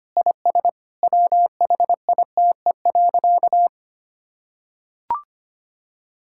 Morse code, heard as a keyed tone, about 25 wpm, 700 Hz. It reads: IS WHITE.
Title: Sentences from Top 1600 Words 25wpm